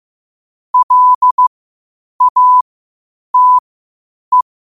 Morse code, heard as a keyed tone, about 15 wpm, spaced as for 8 wpm, 1000 Hz. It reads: LATE
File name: qotdfiles / 8wpm